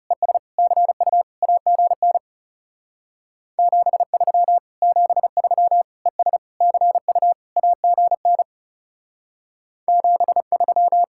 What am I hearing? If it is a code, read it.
Morse code, heard as a keyed tone, about 36 wpm, 700 Hz. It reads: ES CU AGN 73 73 ES CU AGN 73
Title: Speed-Racing Common QSO Phrases Encore - 30wpm